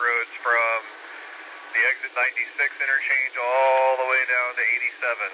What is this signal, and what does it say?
Emergency Alert System (SAME) tones, heard as harsh buzzing bursts, with speech. It is heard from 1.0 to 5.0 s.